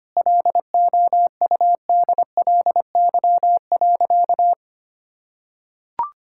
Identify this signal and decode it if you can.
Morse code, heard as a keyed tone, about 25 wpm, 700 Hz. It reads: LOUDLY.
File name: practice / 25wpm / Sentences with 2-Word Phrases - 25wpm